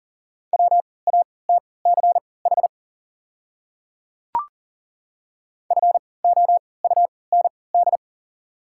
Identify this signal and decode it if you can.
Morse code, heard as a keyed tone, about 40 wpm, spaced as for 23 wpm, 700 Hz. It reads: WATCH FOUND
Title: 3x Character Spacing - Top 500 Words - 40wpm